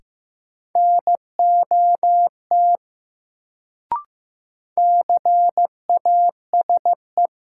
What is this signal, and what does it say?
Morse code, heard as a keyed tone, about 15 wpm, 700 Hz. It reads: NOT CASE
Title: Sets of 2 Words - New Word Drill - 15wpm